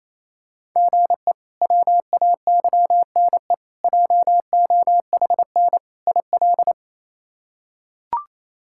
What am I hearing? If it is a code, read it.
Morse code, heard as a keyed tone, about 28 wpm, 700 Hz. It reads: GE WAYNE JOHN IL
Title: K1USN SST QSO Practice - 28wpm